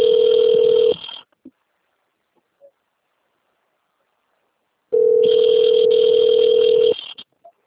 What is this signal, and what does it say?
Ringback tone (North America), 440 + 480 Hz.